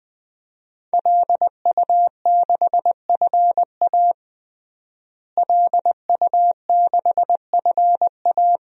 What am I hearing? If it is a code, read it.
Morse code, heard as a keyed tone, about 21 wpm, 700 Hz. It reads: LU6FA LU6FA
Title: DXCC-100 Call Signs - 5 Characters - Repeated 2x - 20wpm